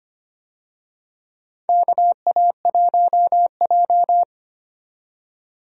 Morse code, heard as a keyed tone, about 25 wpm, 700 Hz. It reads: KA1J
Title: USA Call Signs - 4 Characters - 25wpm